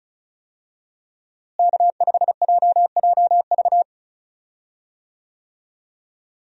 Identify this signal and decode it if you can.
Morse code, heard as a keyed tone, about 34 wpm, 700 Hz. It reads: K5JJV
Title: USA Call Signs - 5 Characters - Repeated 2x - 35wpm